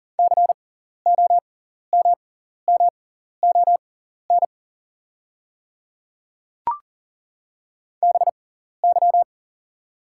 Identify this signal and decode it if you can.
Morse code, heard as a keyed tone, about 40 wpm, spaced as for 14 wpm, 690 Hz. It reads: COMMON BY
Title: 6x Character Spacing - Top 500 Words - 40wpm